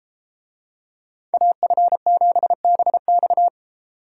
Morse code, heard as a keyed tone, about 30 wpm, 700 Hz. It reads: AF7BX